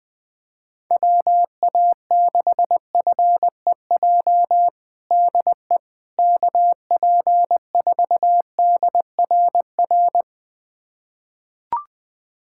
Morse code, heard as a keyed tone, about 20 wpm, 700 Hz. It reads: WA6FEJ DE KP4DRR
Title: USA Callsign DE Callsign - 6 Characters - 20wpm